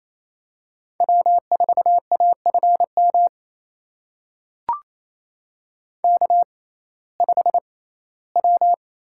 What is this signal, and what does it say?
Morse code, heard as a keyed tone, about 28 wpm, spaced as for 17 wpm, 700 Hz. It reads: W4AFM K5W
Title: USA Call Signs - 5 Characters - 6x Character Spacing - 28wpm